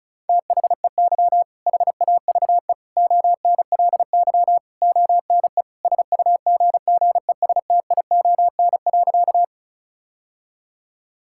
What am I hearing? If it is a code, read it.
Morse code, heard as a keyed tone, about 34 wpm, 700 Hz. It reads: THEY HAVE ONLY ONE SUGGESTION.
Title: Sentences from Top 1600 Words 35wpm